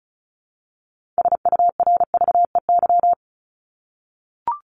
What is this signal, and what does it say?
Morse code, heard as a keyed tone, about 34 wpm, 700 Hz. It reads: SURVEY
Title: Top 1501-1750 Words 35wpm